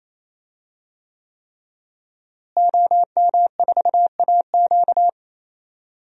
Morse code, heard as a keyed tone, about 28 wpm, 700 Hz. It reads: OM4AQ
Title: DXCC-100 Call Signs - 5 Characters - Repeated 3x - 28wpm